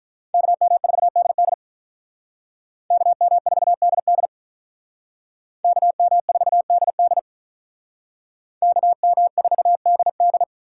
Morse code, heard as a keyed tone, about 45 wpm, 700 Hz. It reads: KM4DD KM4DD KM4DD KM4DD